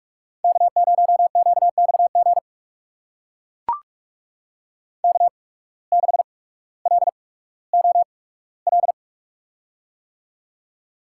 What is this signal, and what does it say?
Morse code, heard as a keyed tone, about 44 wpm, spaced as for 19 wpm, 700 Hz. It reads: K0QXG K6LOL
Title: USA Call Signs - 5 Characters - 8x Character Spacing - 45wpm